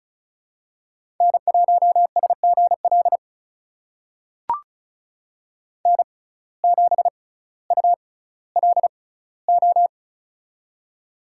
Morse code, heard as a keyed tone, about 36 wpm, spaced as for 18 wpm, 700 Hz. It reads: N1SGL N7ULO
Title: USA Call Signs - 5 Characters - 6x Character Spacing - 35wpm